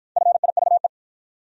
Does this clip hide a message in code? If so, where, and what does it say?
Yes: Morse code from 0.2 to 0.9 s, heard as a keyed tone, about 52 wpm, 700 Hz. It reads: FIVE